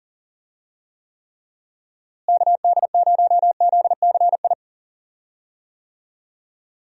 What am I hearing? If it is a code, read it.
Morse code, heard as a keyed tone, about 40 wpm, 700 Hz. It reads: KD0ZCI
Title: USA Call Signs - 6 Characters - Repeated 3x - 40wpm